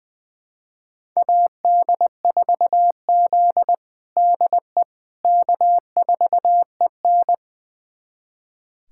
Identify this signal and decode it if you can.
Morse code, heard as a keyed tone, about 20 wpm, 700 Hz. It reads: AD4Z DE K4EN
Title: USA Callsign DE Callsign - 4 Characters - 20wpm